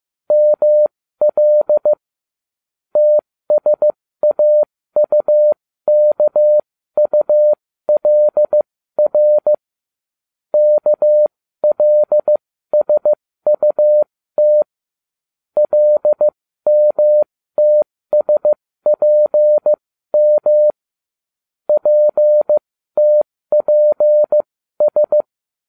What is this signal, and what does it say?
Morse code, heard as a keyed tone, about 15 wpm, spaced as for 13 wpm, 600 Hz. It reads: ML TSAUKULR KLSUT LMTSPM PTPS